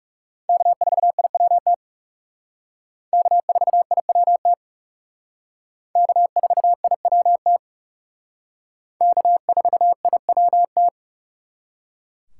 Morse code, heard as a keyed tone, about 36 wpm, 700 Hz. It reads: K4IWT K4IWT K4IWT K4IWT
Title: Speed Racing - USA 1x3 Call Signs 30wpm